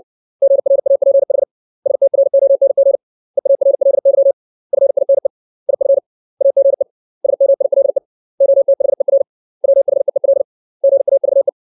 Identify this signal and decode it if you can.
Morse code, heard as a keyed tone, about 60 wpm, 550 Hz. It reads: GRAPH STRONG EARLY FINE IF AGE SMILE OTHER WHEEL GAVE
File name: 632 words / 632 Words 60WPM